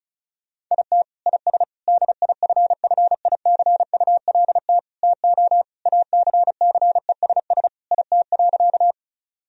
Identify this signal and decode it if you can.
Morse code, heard as a keyed tone, about 36 wpm, 700 Hz. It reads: IT IS DIFFICULT TO ACCESS IT.
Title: Sentences from Top 1000 Words - Review Part 4 - 35wpm